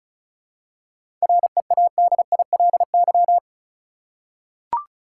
Morse code, heard as a keyed tone, about 34 wpm, 700 Hz. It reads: READILY
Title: Seven Letter Words - 35wpm